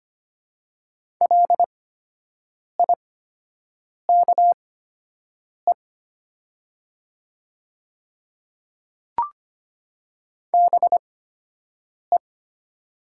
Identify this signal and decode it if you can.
Morse code, heard as a keyed tone, about 25 wpm, spaced as for 7 wpm, 700 Hz. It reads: LIKE BE
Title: 8x Character Spacing - Top 300 Words - 25wpm